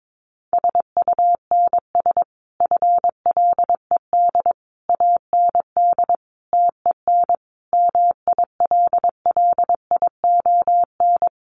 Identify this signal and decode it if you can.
Morse code, heard as a keyed tone, about 22 wpm, 700 Hz. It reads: SUNS FLED AND TEN MILLION
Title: A Sound Of Thunder 22wpm